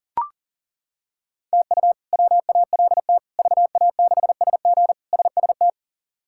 Morse code, heard as a keyed tone, about 40 wpm, 700 Hz. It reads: TU WALT VA6SG SST
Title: K1USN SST QSO Practice - 40wpm